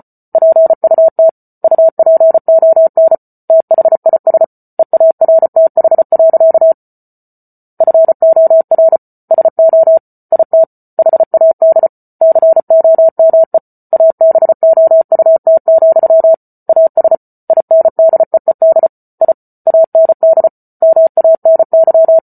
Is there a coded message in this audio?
Yes: Morse code from 0.3 to 22.2 s, heard as a keyed tone, about 34 wpm, 650 Hz. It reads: PUT UPON THIS EARTH. FOR SO IT HAD COME ABOUT, AS INDEED I AND MANY